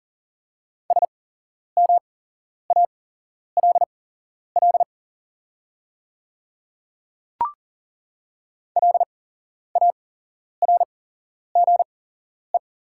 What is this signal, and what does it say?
Morse code, heard as a keyed tone, about 41 wpm, spaced as for 11 wpm, 700 Hz. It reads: SMALL LARGE
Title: 8x Character Spacing - Top 200 Words - 40wpm